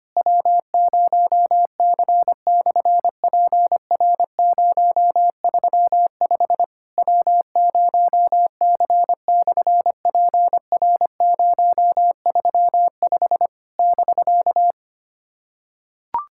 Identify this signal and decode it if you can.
Morse code, heard as a keyed tone, about 24 wpm, 700 Hz. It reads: W0C/PR035 W0C/PR035 <BK>
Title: SOTA QSO Practice 25wpm